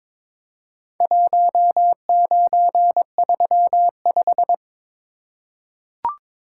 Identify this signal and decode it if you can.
Morse code, heard as a keyed tone, about 23 wpm, 700 Hz. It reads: 1935
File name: practice / 22wpm / Four Letter-Number Combinations - 22wpm